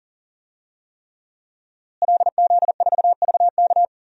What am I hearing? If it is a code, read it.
Morse code, heard as a keyed tone, about 37 wpm, 700 Hz. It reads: LZ4VK
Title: DXCC-100 Call Signs - 5 Characters - Repeated 3x - 40wpm